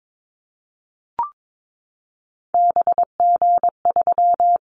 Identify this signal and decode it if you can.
Morse code, heard as a keyed tone, about 22 wpm, 700 Hz. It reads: BG3
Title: Callsign Prefixes for Top 100 DXCC - 22wpm